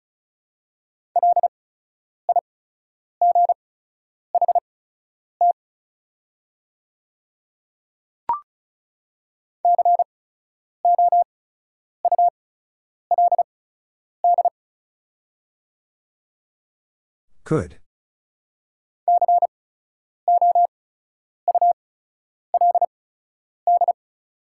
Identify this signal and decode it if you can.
Morse code, heard as a keyed tone, about 36 wpm, spaced as for 10 wpm, 700 Hz. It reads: LIGHT COULD COULD
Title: 8x Character Spacing - Top 300 Words - 35wpm